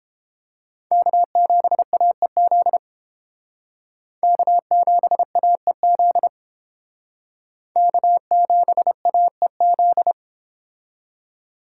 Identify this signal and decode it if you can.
Morse code, heard as a keyed tone, about 30 wpm, 700 Hz. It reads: K7AEZ K7AEZ K7AEZ